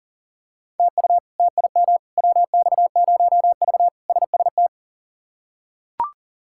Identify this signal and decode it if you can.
Morse code, heard as a keyed tone, about 40 wpm, 700 Hz. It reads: TU TIM WX0V SST
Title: K1USN SST QSO Practice - 40wpm